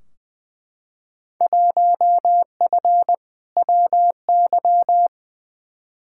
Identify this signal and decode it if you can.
Morse code, heard as a keyed tone, about 20 wpm, 700 Hz. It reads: 1F WY
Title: Field Day Exchanges Encore - 20wpm